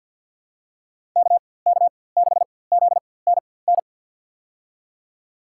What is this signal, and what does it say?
Morse code, heard as a keyed tone, about 46 wpm, spaced as for 23 wpm, 700 Hz. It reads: KK6ZNN
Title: USA Call Signs - 6 Characters - 4x Character Spacing - 50wpm